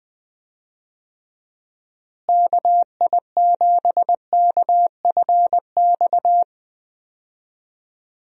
Morse code, heard as a keyed tone, about 21 wpm, 700 Hz. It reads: KI7KFX
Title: USA Call Signs - 6 Characters - Repeated 3x - 20wpm